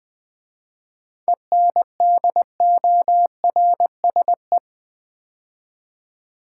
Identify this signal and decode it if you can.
Morse code, heard as a keyed tone, about 20 wpm, 700 Hz. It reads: ENDORSE
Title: 7 Letter Words - First 3 Letters Given - 20wpm